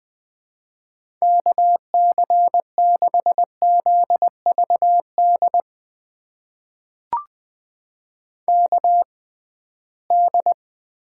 Morse code, heard as a keyed tone, about 21 wpm, spaced as for 15 wpm, 700 Hz. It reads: KC6ZVD KD
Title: USA Call Signs - 6 Characters - 6x Character Spacing - 20wpm